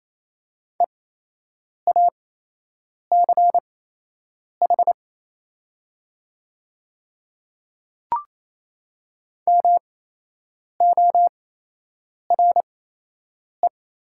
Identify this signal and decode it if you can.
Morse code, heard as a keyed tone, about 28 wpm, spaced as for 8 wpm, 700 Hz. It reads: EACH MORE
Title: 8x Character Spacing - Top 100 Words - 28wpm